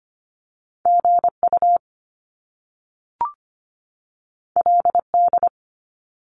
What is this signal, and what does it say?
Morse code, heard as a keyed tone, about 25 wpm, 700 Hz. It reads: GU LD
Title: Top Two Letter Combinations in Words 25wpm